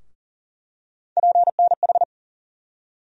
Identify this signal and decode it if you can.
Morse code, heard as a keyed tone, about 40 wpm, 700 Hz. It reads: PNH
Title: USA Call Sign Postfixes - 3 Letters 40wpm